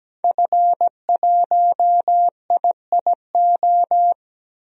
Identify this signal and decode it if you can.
Morse code, heard as a keyed tone, about 17 wpm, 700 Hz. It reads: F1IIO
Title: DXCC-100 Call Signs - 5 Characters - 17wpm